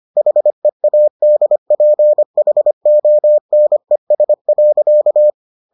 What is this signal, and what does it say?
Morse code, heard as a keyed tone, about 25 wpm, 600 Hz. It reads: HEADPHONES.